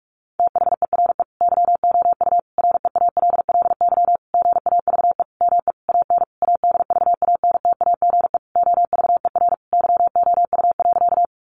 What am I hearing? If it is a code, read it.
Morse code, heard as a keyed tone, about 44 wpm, 700 Hz. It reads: THERE YOU REALLY GAVE ME AN ADVANTAGE OVER YOU.
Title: Sentences from Top 900 Words 45wpm